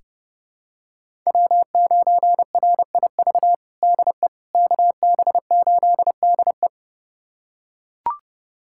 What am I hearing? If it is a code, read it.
Morse code, heard as a keyed tone, about 30 wpm, 700 Hz. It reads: W9RIV DE KB8DE
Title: USA Callsign DE Callsign - 5 Characters - 30wpm